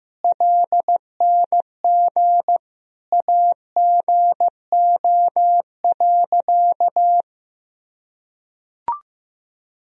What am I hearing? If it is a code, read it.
Morse code, heard as a keyed tone, about 15 wpm, 700 Hz. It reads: LNG AGO.
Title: Sentences from Top 500 Words and Abbreviations 15wpm